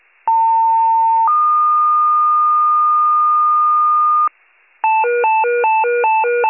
Two-tone page, heard as a one-second tone, then a three-second tone, about 900 then 1220 Hz. It is heard from 0.3 to 4.3 s.